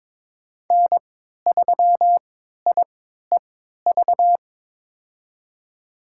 Morse code, heard as a keyed tone, about 22 wpm, spaced as for 13 wpm, 700 Hz. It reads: N3IEV
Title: USA Call Signs - 5 Characters - 3x Character Spacing - 22wpm